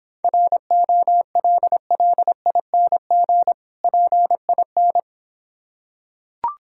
Morse code, heard as a keyed tone, about 26 wpm, 690 Hz. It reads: ROLLING PIN